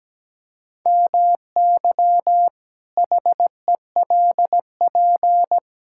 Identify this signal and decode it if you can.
Morse code, heard as a keyed tone, about 17 wpm, 700 Hz. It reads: MY HELP
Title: Pangram Challenge v2 - 17wpm